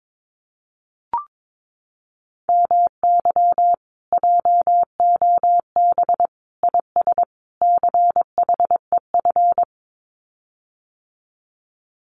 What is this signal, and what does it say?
Morse code, heard as a keyed tone, about 22 wpm, 700 Hz. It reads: MY JOB IS CHEF
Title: Common QSO Phrases Encore 22wpm